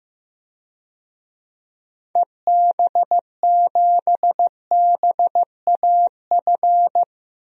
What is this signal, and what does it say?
Morse code, heard as a keyed tone, about 15 wpm, 700 Hz. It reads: EB7BAF
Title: DXCC-100 Call Signs - 6 Characters - Repeated 3x - 15wpm